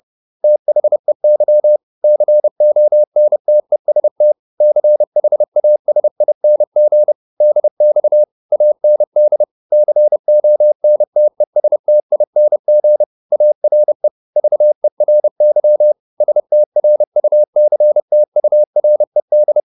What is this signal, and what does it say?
Morse code, heard as a keyed tone, about 30 wpm, 600 Hz. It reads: THEY CONTEST CHASING DX AND CONTESTING ARE VERY STRUCTURED